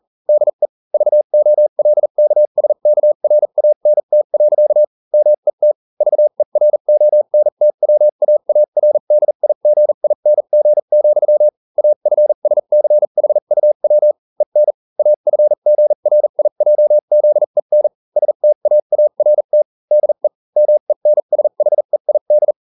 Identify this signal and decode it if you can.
Morse code, heard as a keyed tone, about 40 wpm, spaced as for 35 wpm, 600 Hz. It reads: DE VOLKSKRANT. MET VERONTWAARDIGING, AFSCHUW EN AFGRIJZEN STAART DE MENSHEID